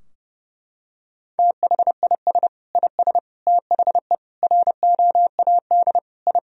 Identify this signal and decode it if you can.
Morse code, heard as a keyed tone, about 30 wpm, 700 Hz. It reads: THIS IS THE ROAD I
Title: Sentences from Top 1000 Words - Review Part 2 - 30wpm